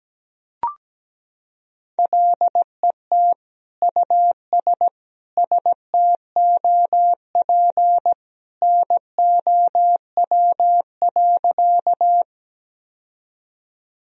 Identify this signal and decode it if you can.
Morse code, heard as a keyed tone, about 17 wpm, 700 Hz. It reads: LET US STOP NOW.
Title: Sentences from Top 1000 Words - Review Part 1 - 17wpm